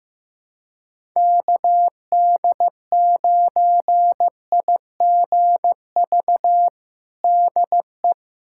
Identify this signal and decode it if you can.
Morse code, heard as a keyed tone, about 16 wpm, 700 Hz. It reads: KD9IGV DE
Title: USA Callsign DE Callsign - 6 Characters - 15wpm